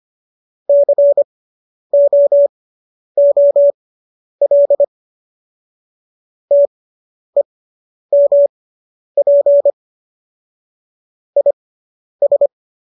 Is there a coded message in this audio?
Yes: Morse code from 0.7 to 12.5 s, heard as a keyed tone, about 25 wpm, spaced as for 10 wpm, 570 Hz. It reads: COOL TEMP IS